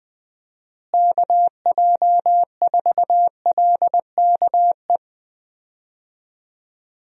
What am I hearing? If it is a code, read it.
Morse code, heard as a keyed tone, about 21 wpm, 700 Hz. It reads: KJ4LKE
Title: USA Call Signs - 6 Characters - Repeated 2x - 20wpm